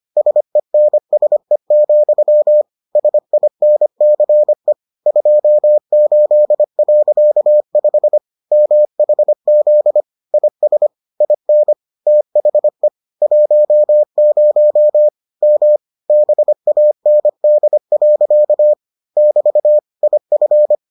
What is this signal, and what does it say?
Morse code, heard as a keyed tone, about 25 wpm, 600 Hz. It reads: SENSE, SINCE 28.5 MHZ IS IN THE 10 M BAND. <BT> IF